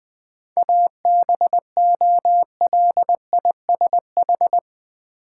Morse code, heard as a keyed tone, about 20 wpm, 700 Hz. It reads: ABOLISH
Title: 7 Letter Words - First 3 Letters Given - 20wpm